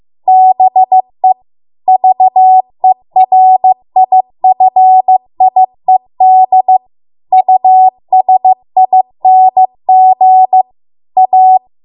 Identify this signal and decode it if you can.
Morse code, heard as a keyed tone, about 15 wpm, 750 Hz. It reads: BE VERIFIED USING A